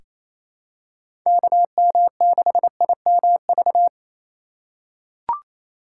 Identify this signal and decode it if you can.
Morse code, heard as a keyed tone, about 28 wpm, 700 Hz. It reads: KM6IMV
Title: USA Call Signs - 6 Characters - 3x Character Spacing - 28wpm